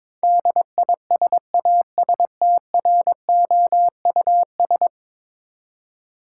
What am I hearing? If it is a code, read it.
Morse code, heard as a keyed tone, about 22 wpm, 700 Hz. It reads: DISASTROUS
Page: morse-code-ninja-practice-22wpm